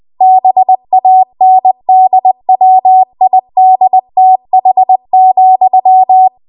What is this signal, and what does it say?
Morse code, heard as a keyed tone, about 20 wpm, 750 Hz. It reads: BANDWIDTH,